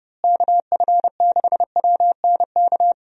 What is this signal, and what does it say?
Morse code, heard as a keyed tone, about 30 wpm, 700 Hz. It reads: KF6WNK
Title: USA Call Signs - 6 Characters - Repeated 3x - 30wpm